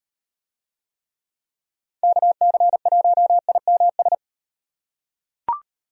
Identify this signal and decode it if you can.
Morse code, heard as a keyed tone, about 38 wpm, 700 Hz. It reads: KC1IMS